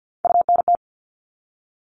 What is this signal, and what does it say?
Morse code, heard as a keyed tone, about 50 wpm, 700 Hz. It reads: UNT
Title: All Cut Numbers - 3-Digit 50wpm